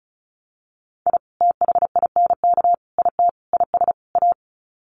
Morse code, heard as a keyed tone, about 34 wpm, 700 Hz. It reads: I THINK IT IS A
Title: Sets of 5 Words Encore 35wpm